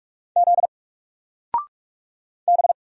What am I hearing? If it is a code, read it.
Morse code, heard as a keyed tone, about 44 wpm, 700 Hz. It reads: Z B
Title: Single Character (Letter:Number:Punctuation:Prosign) 45wpm